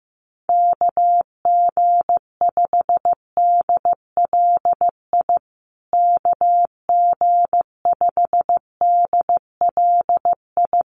Morse code, heard as a keyed tone, about 15 wpm, 700 Hz. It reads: KG5DLI KG5DLI
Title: USA Call Signs Repeated 3x 15wpm